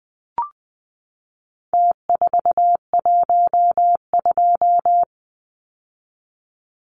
Morse code, heard as a keyed tone, about 20 wpm, 700 Hz. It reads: T412
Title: Numbers - Contest Style - 4-Digit 20wpm